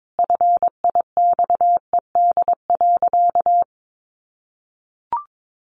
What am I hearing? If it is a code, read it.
Morse code, heard as a keyed tone, about 22 wpm, 700 Hz. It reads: FIXED.
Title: Sentences from Top 1100 Words 22wpm